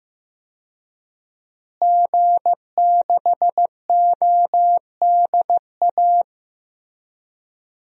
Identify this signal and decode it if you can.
Morse code, heard as a keyed tone, about 15 wpm, 700 Hz. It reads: G6ODA
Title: DXCC-100 Call Signs - 5 Characters - Repeated 2x - 15wpm